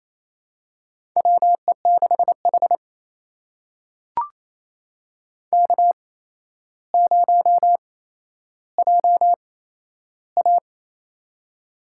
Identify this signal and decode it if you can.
Morse code, heard as a keyed tone, about 28 wpm, spaced as for 12 wpm, 700 Hz. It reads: WE6H K0JA
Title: USA Call Signs - 4 Characters - 8x Character Spacing - 28wpm